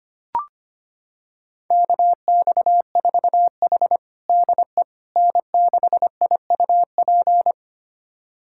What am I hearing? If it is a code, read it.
Morse code, heard as a keyed tone, about 25 wpm, 700 Hz. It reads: KX4H DE N6IUP
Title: USA Callsign DE Callsign 25wpm